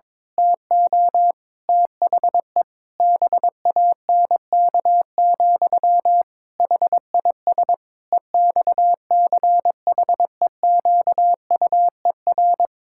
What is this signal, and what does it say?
Morse code, heard as a keyed tone, about 22 wpm, 700 Hz. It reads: TO THE BANK, HIS EXCHEQUER
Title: Pangram Challenge v2 - 22wpm